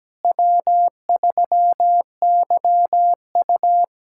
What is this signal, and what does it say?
Morse code, heard as a keyed tone, about 17 wpm, 700 Hz. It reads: W3YU
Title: USA Callsign DE Callsign - 4 Characters - 17wpm